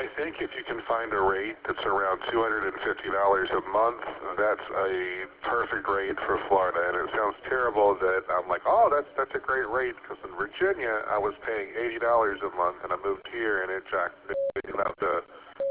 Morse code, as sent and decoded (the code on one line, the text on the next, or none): none